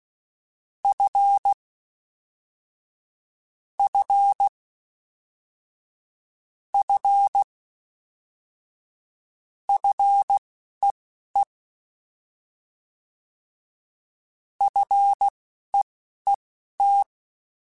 Morse code, heard as a keyed tone, about 16 wpm, spaced as for 11 wpm, 800 Hz. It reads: F F F FEE FEET
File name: Fs / Single F mp3s / Single F Prefixes 3x2 Twice @10